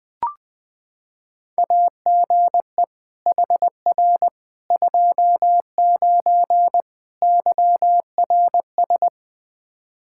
Morse code, heard as a keyed tone, about 21 wpm, 700 Hz. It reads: AGE HR 29 YRS
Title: Practice QSO - Exchange 3 - More Info - 20wpm